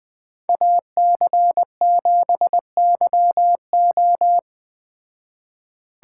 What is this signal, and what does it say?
Morse code, heard as a keyed tone, about 19 wpm, 700 Hz. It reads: AC7YO